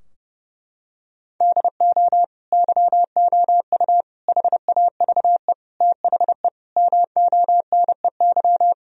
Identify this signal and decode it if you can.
Morse code, heard as a keyed tone, about 30 wpm, 700 Hz. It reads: DO YOU HAVE THE MONEY
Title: Sentences from Top 1000 Words - Review Part 2 - 30wpm